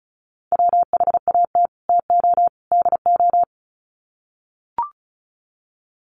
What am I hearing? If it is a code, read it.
Morse code, heard as a keyed tone, about 34 wpm, 700 Hz. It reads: WHAT TO DO
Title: Sets of 3 Words 35wpm